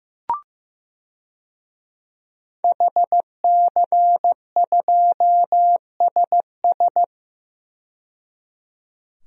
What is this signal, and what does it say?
Morse code, heard as a keyed tone, about 15 wpm, 700 Hz. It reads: HC2SS
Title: DXCC-100 Call Signs - 5 Characters - 15wpm